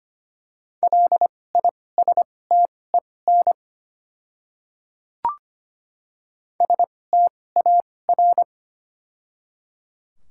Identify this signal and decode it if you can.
Morse code, heard as a keyed tone, about 25 wpm, spaced as for 18 wpm, 700 Hz. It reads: LISTEN STAR
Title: 2x Character Spacing - Top 500 Words - 25wpm